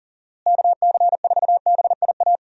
Morse code, heard as a keyed tone, about 40 wpm, 700 Hz. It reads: KC4BIA
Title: USA Call Signs - 6 Characters - Repeated 2x - 40wpm